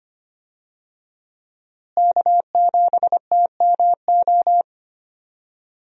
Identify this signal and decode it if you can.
Morse code, heard as a keyed tone, about 25 wpm, 700 Hz. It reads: K7TMO
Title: USA Call Signs - 5 Characters - Repeated 2x - 25wpm